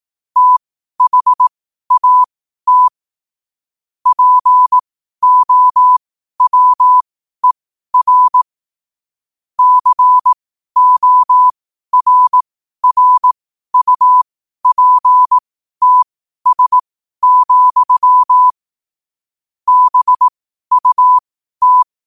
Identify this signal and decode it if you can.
Morse code, heard as a keyed tone, about 18 wpm, spaced as for 12 wpm, 1000 Hz. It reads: THAT POWER CORRUPTS, BUT
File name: qotdfiles / 12wpm